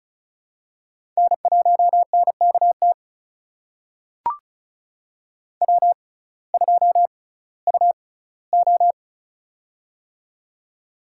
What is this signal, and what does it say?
Morse code, heard as a keyed tone, about 34 wpm, spaced as for 19 wpm, 700 Hz. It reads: N1NKT W2UO
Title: USA Call Signs - 6x Character Spacing - 35wpm